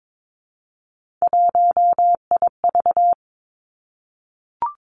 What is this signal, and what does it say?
Morse code, heard as a keyed tone, about 22 wpm, 700 Hz. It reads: 1IV
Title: Three Challenging Letter-Number Combinations 22wpm